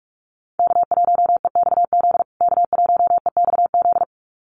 Morse code, heard as a keyed tone, about 44 wpm, 690 Hz. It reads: K1EXZ K1EXZ
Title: USA Call Signs Repeated 2x 45wpm